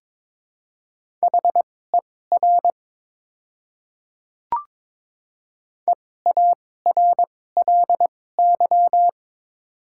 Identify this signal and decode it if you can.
Morse code, heard as a keyed tone, about 22 wpm, spaced as for 16 wpm, 690 Hz. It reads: HER EARLY
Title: 2x Character Spacing - Top 500 Words - 22wpm